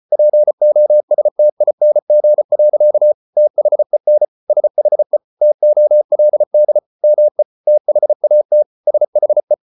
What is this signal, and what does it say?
Morse code, heard as a keyed tone, about 34 wpm, 600 Hz. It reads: POSTING. THEN SHE TOLD ME THAT SHE